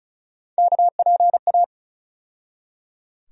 Morse code, heard as a keyed tone, about 35 wpm, 700 Hz. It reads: KPA
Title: USA Call Sign Postfixes - 3 Letters 35wpm